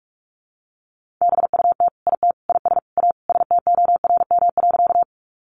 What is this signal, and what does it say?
Morse code, heard as a keyed tone, about 44 wpm, 700 Hz. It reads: BUT IT IS A STORM.
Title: Sentences from Top 1500 Words 45wpm